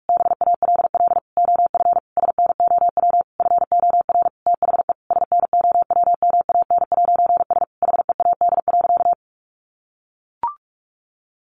Morse code, heard as a keyed tone, about 44 wpm, 700 Hz. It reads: BALL OF SNOW FOR THE SNOWMAN'S HEAD.
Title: Easy Phrases 45wpm